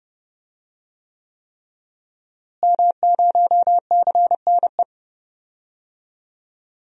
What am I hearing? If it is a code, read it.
Morse code, heard as a keyed tone, about 30 wpm, 700 Hz. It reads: M0CNE
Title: DXCC-100 Call Signs - 5 Characters - Repeated 3x - 30wpm